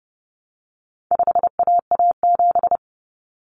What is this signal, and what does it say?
Morse code, heard as a keyed tone, about 30 wpm, 700 Hz. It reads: 5AA7
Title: Common Cut Numbers - 4-Digit 30wpm